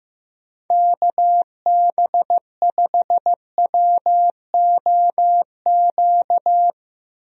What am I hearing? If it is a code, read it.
Morse code, heard as a keyed tone, about 15 wpm, 700 Hz. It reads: KB5WOQ
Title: USA Call Signs - 6 Characters - Repeated 3x - 15wpm